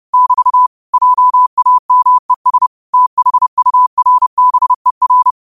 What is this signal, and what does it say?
Morse code, heard as a keyed tone, about 30 wpm, 1000 Hz. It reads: <BT> JAMES THURBER
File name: qotdfiles / 30wpm